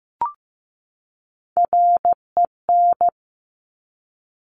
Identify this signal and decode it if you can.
Morse code, heard as a keyed tone, about 15 wpm, 700 Hz. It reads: REN
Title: Top Three Letter Combinations in Words 15wpm